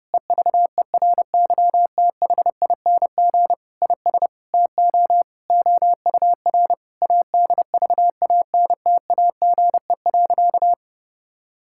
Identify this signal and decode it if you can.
Morse code, heard as a keyed tone, about 32 wpm, 700 Hz. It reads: EVERYTHING IS TO OUR ADVANTAGE.
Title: Sentences from Top 1000 Words - Review Part 6 - 30wpm